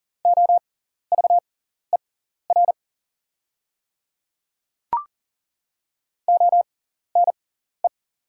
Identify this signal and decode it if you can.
Morse code, heard as a keyed tone, about 40 wpm, spaced as for 14 wpm, 700 Hz. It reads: OVER ONE
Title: 6x Character Spacing - Top 200 Words - 40wpm